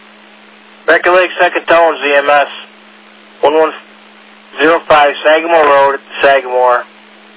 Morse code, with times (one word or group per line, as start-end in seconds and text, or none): none